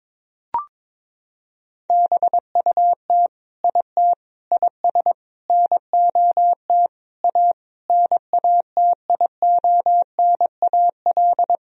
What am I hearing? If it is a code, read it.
Morse code, heard as a keyed tone, about 22 wpm, 700 Hz. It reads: BUT IT IS NOT A NATIONAL
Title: Sentences from Top 1000 Words - Review Part 4 - 22wpm